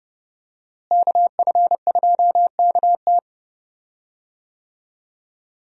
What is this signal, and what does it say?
Morse code, heard as a keyed tone, about 30 wpm, 700 Hz. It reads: KF2KT
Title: USA Call Signs - 5 Characters - Repeated 2x - 30wpm